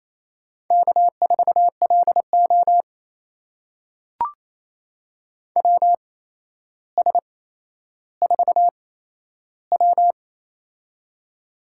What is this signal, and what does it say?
Morse code, heard as a keyed tone, about 28 wpm, spaced as for 12 wpm, 700 Hz. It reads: K4LO WS4W
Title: USA Call Signs - 4 Characters - 8x Character Spacing - 28wpm